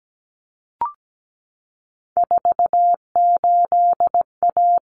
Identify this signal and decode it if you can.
Morse code, heard as a keyed tone, about 17 wpm, 700 Hz. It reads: 48A
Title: Common Cut Numbers - 3-Digit 17wpm